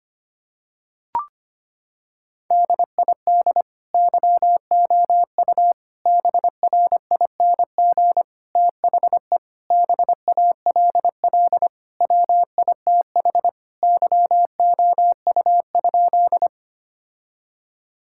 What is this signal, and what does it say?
Morse code, heard as a keyed tone, about 25 wpm, 700 Hz. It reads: DID YOU BRING THE BALL WITH YOU?
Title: Sentences from Top 1000 Words - Review Part 6 - 25wpm